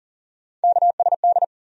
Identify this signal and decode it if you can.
Morse code, heard as a keyed tone, about 40 wpm, 700 Hz. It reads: KSD
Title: USA Call Sign Postfixes - 3 Letters 40wpm